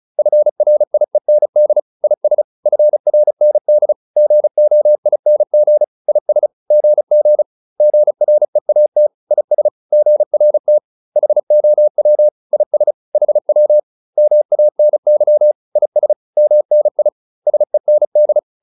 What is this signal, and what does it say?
Morse code, heard as a keyed tone, about 34 wpm, 600 Hz. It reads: FRIEND IS FRND GOING IS GG GREAT IS GRT HOW IS HW MANY IS MNI SEND